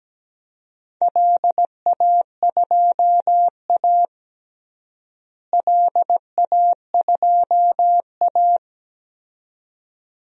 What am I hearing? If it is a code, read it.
Morse code, heard as a keyed tone, about 17 wpm, 700 Hz. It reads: LA2A LA2A
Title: DXCC-100 Call Signs - 4 Characters - Repeated 2x - 17wpm